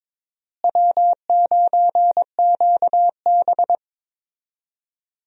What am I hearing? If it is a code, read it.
Morse code, heard as a keyed tone, about 22 wpm, 700 Hz. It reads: W9QB
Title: USA Call Signs - 4 Characters - Repeated 2x - 22wpm